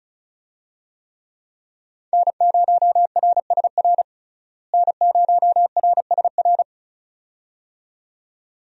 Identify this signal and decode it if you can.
Morse code, heard as a keyed tone, about 36 wpm, 700 Hz. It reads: N0RSR N0RSR
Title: USA Call Signs - 5 Characters - Repeated 2x - 35wpm